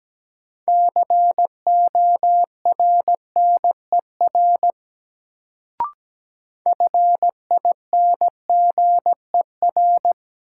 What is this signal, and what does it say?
Morse code, heard as a keyed tone, about 17 wpm, 700 Hz. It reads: CORNER FINGER
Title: Sentences from Top 1600 Words - New Word Drill Alt - 17wpm